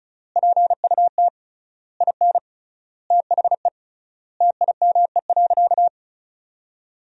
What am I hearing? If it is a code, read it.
Morse code, heard as a keyed tone, about 36 wpm, 700 Hz. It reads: PUT IN THE TIME.
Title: 3x Word-Spacing - Sentences from Top 200 Words - 35wpm